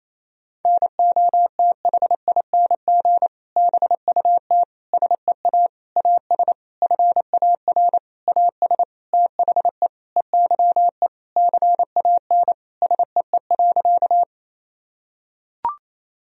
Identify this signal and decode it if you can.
Morse code, heard as a keyed tone, about 28 wpm, 700 Hz. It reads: NOTHING BUT SEA AS FAR AS THE EYE CAN SEE.
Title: Sentences from Top 1000 Words - Review Part 1 - 28wpm